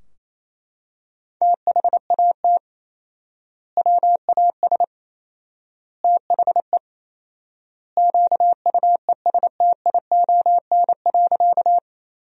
Morse code, heard as a keyed tone, about 28 wpm, 700 Hz. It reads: THAT WAS THE QUESTION.
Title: 4x Word-Spacing - Sentences from Top 400 Words - 28wpm